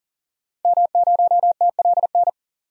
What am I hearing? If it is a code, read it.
Morse code, heard as a keyed tone, about 40 wpm, 700 Hz. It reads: M0TLN
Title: DXCC-100 Call Signs - 5 Characters - 40wpm